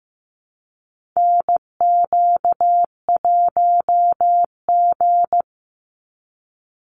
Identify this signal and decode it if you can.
Morse code, heard as a keyed tone, about 15 wpm, 700 Hz. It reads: NQ1G
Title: USA Call Signs 15wpm